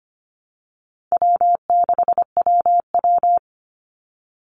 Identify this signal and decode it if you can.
Morse code, heard as a keyed tone, about 25 wpm, 700 Hz. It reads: W6WW
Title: NA SOTA Chaser Callsigns 25wpm